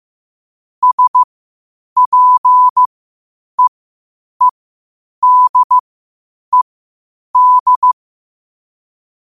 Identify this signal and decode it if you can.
Morse code, heard as a keyed tone, about 15 wpm, spaced as for 8 wpm, 1000 Hz. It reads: SPEEDED
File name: qotdfiles / 8wpm